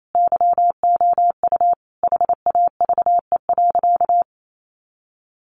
Morse code, heard as a keyed tone, about 28 wpm, 700 Hz. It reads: YOU HAVE.
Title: Sentences from Top 1300 Words 28wpm